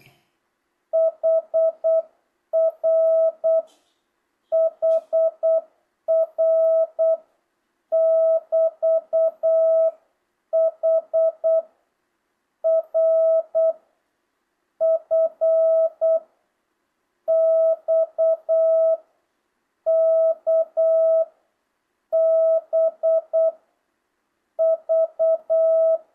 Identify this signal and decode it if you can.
Morse code, heard as a keyed tone, about 8 wpm, 660 Hz. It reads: HR HR <BT> H R F X K B V
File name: jh8cbh / 2025cw